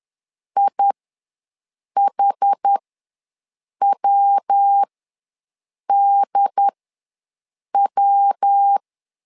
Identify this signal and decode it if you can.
Morse code, heard as a keyed tone, about 10 wpm, spaced as for 6 wpm, 800 Hz. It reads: IHWDW
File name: LEZIONI / 3 MWRO / 3 MWRO ws 10wpm 2310